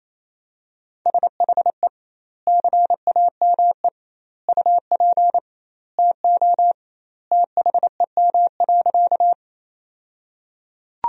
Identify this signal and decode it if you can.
Morse code, heard as a keyed tone, about 28 wpm, 700 Hz. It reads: SHE CAME UP TO THEM.
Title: 2x Word-Spacing - Sentences from Top 200 Words - 28wpm